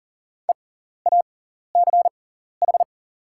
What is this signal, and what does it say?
Morse code, heard as a keyed tone, about 40 wpm, spaced as for 14 wpm, 700 Hz. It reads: EACH
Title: 6x Character Spacing - Top 300 Words - 40wpm